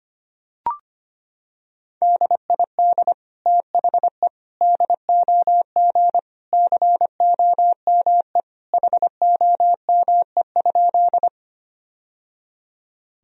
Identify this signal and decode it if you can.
Morse code, heard as a keyed tone, about 25 wpm, 700 Hz. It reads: DID THE DOG COME HOME?
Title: Sentences from Top 1000 Words - Review Part 2 - 25wpm